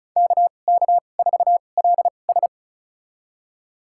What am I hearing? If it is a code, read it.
Morse code, heard as a keyed tone, about 35 wpm, spaced as for 25 wpm, 700 Hz. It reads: KK4LS
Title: USA Call Signs - 5 Characters - 2x Character Spacing - 35wpm